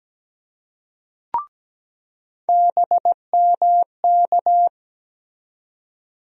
Morse code, heard as a keyed tone, about 17 wpm, 700 Hz. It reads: BMK
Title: USA Call Sign Postfixes - 3 Letters 17wpm